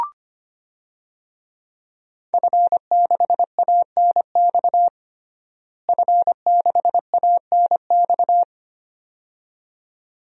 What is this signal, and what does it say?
Morse code, heard as a keyed tone, about 25 wpm, 700 Hz. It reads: F6ANX F6ANX
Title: DXCC-100 Call Signs - 5 Characters - Repeated 2x - 25wpm